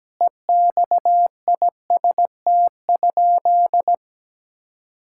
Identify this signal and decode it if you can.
Morse code, heard as a keyed tone, about 17 wpm, 700 Hz. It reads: EXIST?
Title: Sentences from Top 1000 Words - Review Part 5 - 17wpm